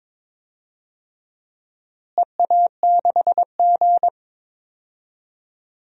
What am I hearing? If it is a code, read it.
Morse code, heard as a keyed tone, about 22 wpm, 700 Hz. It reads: EA6G
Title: DXCC-100 Call Signs - 4 Characters - Repeated 2x - 22wpm